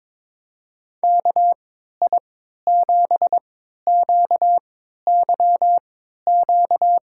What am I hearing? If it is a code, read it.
Morse code, heard as a keyed tone, about 21 wpm, spaced as for 12 wpm, 700 Hz. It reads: KI7QYQ
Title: USA Call Signs - 3x Character Spacing - 22wpm